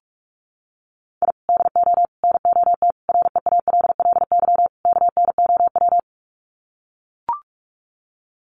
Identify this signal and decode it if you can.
Morse code, heard as a keyed tone, about 44 wpm, 700 Hz. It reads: I DO NOT REALLY KNOW
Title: Sets of 5 Words Encore 45wpm